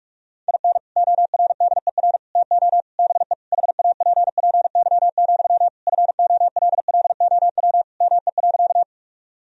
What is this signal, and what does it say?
Morse code, heard as a keyed tone, about 43 wpm, 700 Hz. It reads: IN ORDER TO BE HAPPY, FOLLOW ME.
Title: Sentences from Top 1000 Words - Review Part 2 - 45wpm